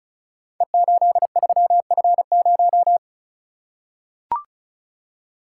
Morse code, heard as a keyed tone, about 36 wpm, 700 Hz. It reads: E83F0
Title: Code Groups - 5 Letter:Numbers - 35wpm